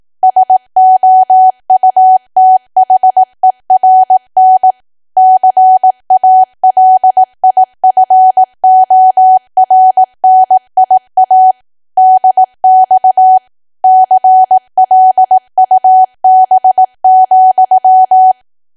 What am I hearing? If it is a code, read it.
Morse code, heard as a keyed tone, about 18 wpm, 750 Hz. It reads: SOUTHERN CALIFORNIA DX CLUB,
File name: Archive / 18 WPM / 18WPM